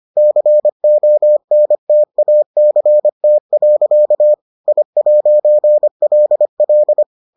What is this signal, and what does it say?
Morse code, heard as a keyed tone, about 25 wpm, 600 Hz. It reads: CONTACT. I'LL